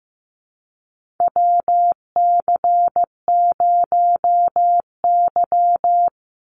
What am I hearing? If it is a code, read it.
Morse code, heard as a keyed tone, about 15 wpm, 700 Hz. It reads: WC0Y
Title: NA SOTA Chaser Callsigns 15wpm